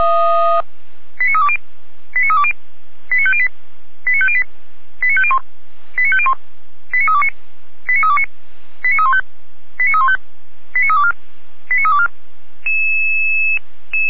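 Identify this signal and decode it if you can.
Five-tone selective call (ZVEI), heard as quick runs of five tones, tones of about 70 ms. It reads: 89320 89320 89508 89508 89501 89501 89329 89329 89315 89315 89324 89324
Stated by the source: ZVEI five-tone selective call received on an SDR, from fire brigade Feuerwehr Orlamünde.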